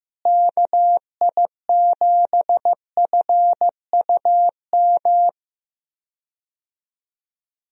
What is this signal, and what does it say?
Morse code, heard as a keyed tone, about 15 wpm, 700 Hz. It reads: KI7FUM
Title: USA Call Signs - 6 Characters - Repeated 2x - 15wpm